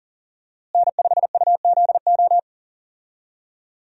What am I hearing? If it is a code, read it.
Morse code, heard as a keyed tone, about 39 wpm, 700 Hz. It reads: N5UZO
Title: USA Call Signs - 5 Characters - Repeated 2x - 40wpm